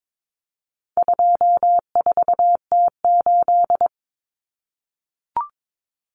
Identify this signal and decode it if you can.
Morse code, heard as a keyed tone, about 22 wpm, 700 Hz. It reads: 24T8
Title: Common Cut Numbers - 4-Digit 22wpm